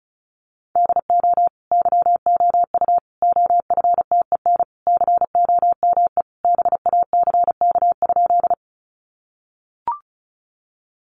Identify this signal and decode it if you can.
Morse code, heard as a keyed tone, about 34 wpm, 700 Hz. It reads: DO YOU OFTEN COME BACK?
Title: Sentences from Top 300 Words 35wpm